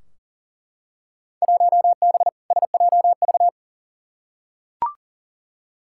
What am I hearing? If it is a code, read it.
Morse code, heard as a keyed tone, about 40 wpm, 700 Hz. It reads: 1B SJV
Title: Field Day Exchanges - 40wpm